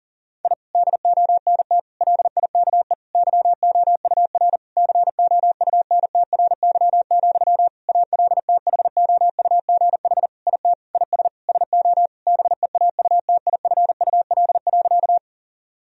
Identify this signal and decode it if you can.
Morse code, heard as a keyed tone, about 40 wpm, 700 Hz. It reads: I DONT LIKE YOUR COUNTRY, ALTHOUGH IT IS SO BEAUTIFUL.
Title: Sentences from Top 1000 Words - Review Part 3 - 40wpm